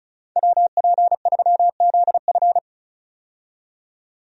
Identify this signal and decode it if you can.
Morse code, heard as a keyed tone, about 36 wpm, 700 Hz. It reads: WP3ZF
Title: USA Call Signs - 5 Characters - 35wpm